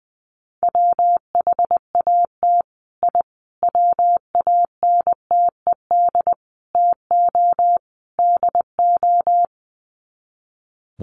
Morse code, heard as a keyed tone, about 20 wpm, 700 Hz. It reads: WHAT I WANTED TO DO
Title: Sets of 5 Words Encore2 20wpm